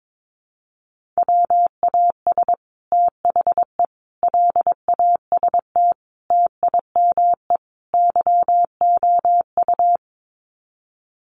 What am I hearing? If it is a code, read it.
Morse code, heard as a keyed tone, about 22 wpm, 700 Hz. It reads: WAS THE LAST TIME YOU
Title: Sets of 5 Words Encore2 22wpm